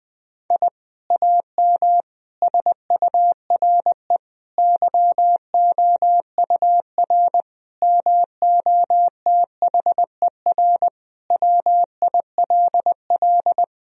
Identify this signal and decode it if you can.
Morse code, heard as a keyed tone, about 20 wpm, 700 Hz. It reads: I AM SURE YOUR MOTHER WILL
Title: Sentences from Top 1000 Words - Review Part 3 - 20wpm